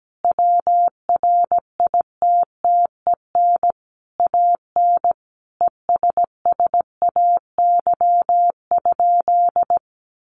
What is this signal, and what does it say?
Morse code, heard as a keyed tone, about 17 wpm, 700 Hz. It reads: WRITTEN AN ESSAY?
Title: Sentences from Top 1500 Words 17wpm